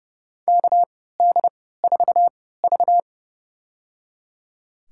Morse code, heard as a keyed tone, about 32 wpm, spaced as for 17 wpm, 700 Hz. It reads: KD4V
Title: USA Call Signs - 4 Characters - 3x Character Spacing - 30wpm